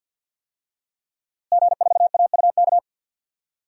Morse code, heard as a keyed tone, about 51 wpm, 700 Hz. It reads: G4AUK
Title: DXCC-100 Call Signs - 5 Characters - Repeated 3x - 50wpm